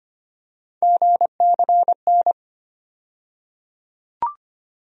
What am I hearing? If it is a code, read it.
Morse code, heard as a keyed tone, about 25 wpm, 700 Hz. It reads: GCN
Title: USA Call Sign Postfixes - 3 Letters 25wpm